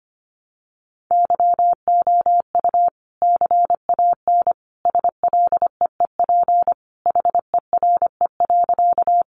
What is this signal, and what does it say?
Morse code, heard as a keyed tone, about 25 wpm, 700 Hz. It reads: YOU CAN SLEEP HERE.
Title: Sentences from Top 500 Words 25wpm